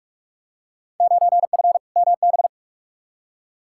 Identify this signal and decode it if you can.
Morse code, heard as a keyed tone, about 42 wpm, 700 Hz. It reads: 9F MB
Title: Field Day Exchanges Encore - 45wpm